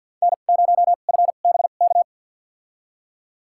Morse code, heard as a keyed tone, about 51 wpm, spaced as for 36 wpm, 700 Hz. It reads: N0FBK
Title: USA Call Signs - 5 Characters - 2x Character Spacing - 50wpm